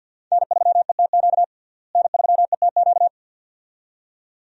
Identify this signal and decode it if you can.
Morse code, heard as a keyed tone, about 53 wpm, 700 Hz. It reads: N3ETQ N3ETQ
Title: USA Call Signs - 5 Characters - Repeated 2x - 50wpm